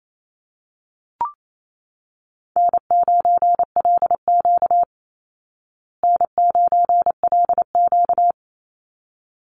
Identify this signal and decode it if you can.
Morse code, heard as a keyed tone, about 28 wpm, 700 Hz. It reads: N9LQ N9LQ
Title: Easy Call Signs Repeated 2x - 28wpm